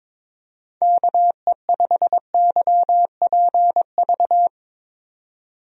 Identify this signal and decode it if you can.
Morse code, heard as a keyed tone, about 21 wpm, 700 Hz. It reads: KE5YPV
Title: USA Call Signs - 6 Characters - 22wpm